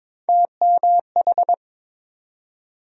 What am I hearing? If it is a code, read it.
Morse code, heard as a keyed tone, about 21 wpm, 700 Hz. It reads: TMH
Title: USA Call Sign Postfixes - 3 Letters 22wpm